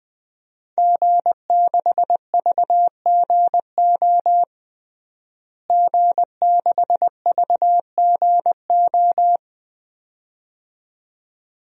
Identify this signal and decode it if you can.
Morse code, heard as a keyed tone, about 20 wpm, 700 Hz. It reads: G6VGO G6VGO
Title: DXCC-100 Call Signs - 5 Characters - Repeated 2x - 20wpm